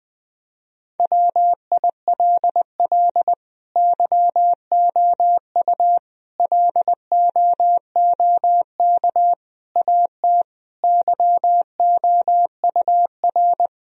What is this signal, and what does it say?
Morse code, heard as a keyed tone, about 19 wpm, 700 Hz. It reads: WILL YOU LOOK AT YOUR
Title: Sentences from Top 1000 Words - Review Part 1 - 20wpm